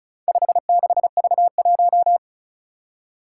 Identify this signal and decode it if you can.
Morse code, heard as a keyed tone, about 36 wpm, 700 Hz. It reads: 56V1
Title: Four Challenging Letter-Number Combinations 35wpm